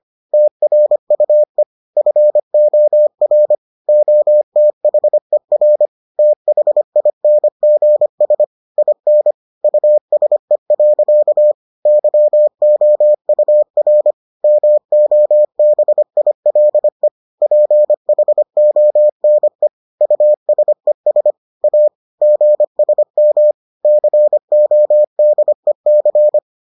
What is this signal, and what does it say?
Morse code, heard as a keyed tone, about 25 wpm, 600 Hz. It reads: TRUE FOR OTHER THINGS IN USE. YOUR MOBILE PHONE USES A GSM CODEC